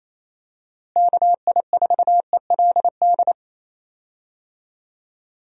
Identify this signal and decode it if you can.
Morse code, heard as a keyed tone, about 27 wpm, 700 Hz. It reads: KI4ELD
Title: USA Call Signs - 6 Characters - Repeated 3x - 28wpm